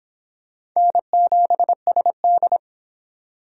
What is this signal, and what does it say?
Morse code, heard as a keyed tone, about 22 wpm, 700 Hz. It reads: N7SD